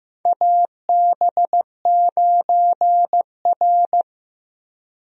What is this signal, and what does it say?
Morse code, heard as a keyed tone, about 15 wpm, 700 Hz. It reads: AB9R
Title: USA Call Signs - 4 Characters - Repeated 3x - 15wpm